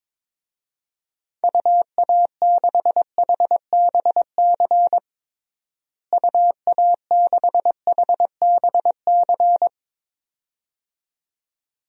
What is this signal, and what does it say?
Morse code, heard as a keyed tone, about 22 wpm, 700 Hz. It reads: UA6HBC UA6HBC
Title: DXCC-100 Call Signs - 6 Characters - Repeated 2x - 22wpm